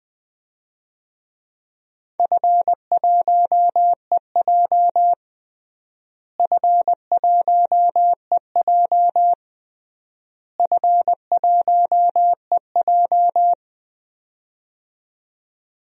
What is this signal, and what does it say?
Morse code, heard as a keyed tone, about 20 wpm, 700 Hz. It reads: F1EJ F1EJ F1EJ
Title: DXCC-100 Call Signs - 4 Characters - Repeated 3x - 20wpm